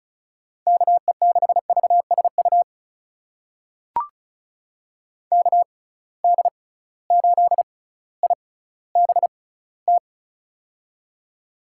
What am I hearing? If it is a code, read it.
Morse code, heard as a keyed tone, about 34 wpm, spaced as for 18 wpm, 700 Hz. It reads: KE6VSU KD8IBT
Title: USA Call Signs - 6 Characters - 6x Character Spacing - 35wpm